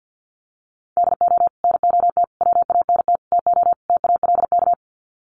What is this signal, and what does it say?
Morse code, heard as a keyed tone, about 50 wpm, 700 Hz. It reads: DO NOT WANT TO TALK
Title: Sets of 5 Words 50wpm